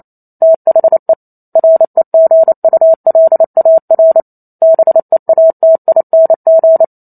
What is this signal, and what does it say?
Morse code, heard as a keyed tone, about 28 wpm, 650 Hz. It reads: THE REGULAR BEATING